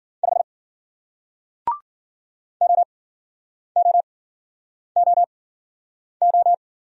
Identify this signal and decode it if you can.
Morse code, heard as a keyed tone, about 50 wpm, spaced as for 9 wpm, 700 Hz. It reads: 5 OOOO